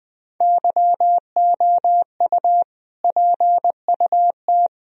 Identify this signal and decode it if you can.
Morse code, heard as a keyed tone, about 20 wpm, 700 Hz. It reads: YOU PUT
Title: Sentences from Top 1000 Words - Review Part 1 - 20wpm